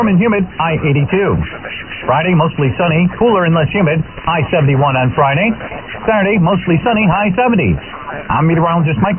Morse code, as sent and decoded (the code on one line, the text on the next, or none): none